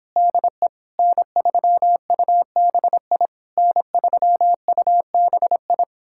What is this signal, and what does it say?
Morse code, heard as a keyed tone, about 29 wpm, 700 Hz. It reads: DE N3UBI N3UBI